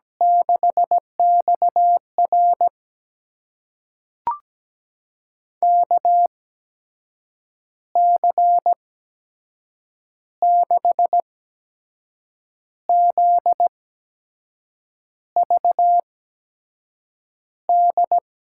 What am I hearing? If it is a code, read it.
Morse code, heard as a keyed tone, about 17 wpm, spaced as for 6 wpm, 700 Hz. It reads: 6XR KC6ZVD
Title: USA Call Signs - 8x Character Spacing - 17wpm